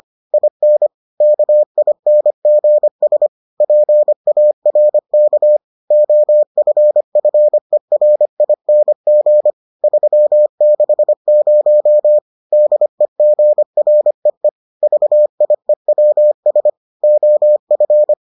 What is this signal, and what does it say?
Morse code, heard as a keyed tone, about 25 wpm, 600 Hz. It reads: IN KINGS PARK OFFERING 360 DEGREE VIEWS OF